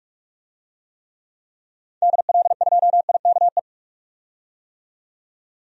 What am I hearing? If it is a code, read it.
Morse code, heard as a keyed tone, about 45 wpm, 700 Hz. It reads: DL2IKE
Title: DXCC-100 Call Signs - 6 Characters - 45wpm